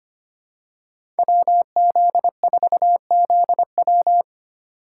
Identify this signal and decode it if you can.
Morse code, heard as a keyed tone, about 25 wpm, 700 Hz. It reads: WZ4ZW
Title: USA Call Signs - 5 Characters - 4x Character Spacing - 25wpm